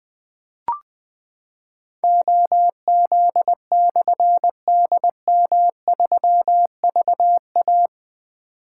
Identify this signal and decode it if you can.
Morse code, heard as a keyed tone, about 21 wpm, 700 Hz. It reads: OZ/DM3VA
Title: Difficult Call Signs 20wpm